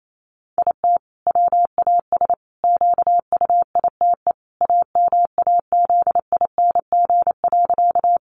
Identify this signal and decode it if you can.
Morse code, heard as a keyed tone, about 28 wpm, 700 Hz. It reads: IT WAS QUITE AMAZING.
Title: Sentences from Top 1500 Words 28wpm